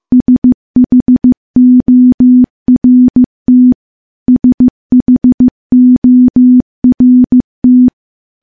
Morse code, heard as a keyed tone, about 15 wpm, 250 Hz. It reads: SHORT SHORT
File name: morse / morse-training-15WPM